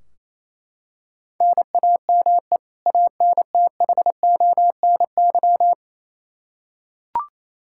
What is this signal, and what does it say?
Morse code, heard as a keyed tone, about 27 wpm, 700 Hz. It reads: NAME ANTHONY
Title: K1USN SST QSO Practice 28wpm